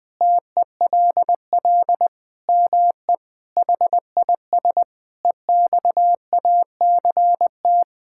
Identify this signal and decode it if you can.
Morse code, heard as a keyed tone, about 21 wpm, 700 Hz. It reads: TELL ME HIS EXACT
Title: Sentences from Top 1600 Words 20wpm